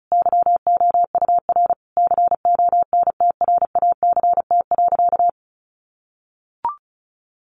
Morse code, heard as a keyed tone, about 34 wpm, 700 Hz. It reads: YOUR CONTRACT.
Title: Sentences from Top 1400 Words 35wpm